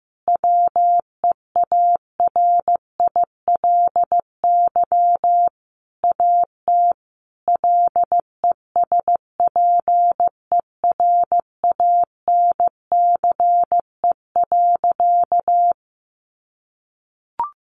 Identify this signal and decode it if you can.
Morse code, heard as a keyed tone, about 15 wpm, 700 Hz. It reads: WEARILY AT LESPERANCE.
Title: A Sound Of Thunder 15wpm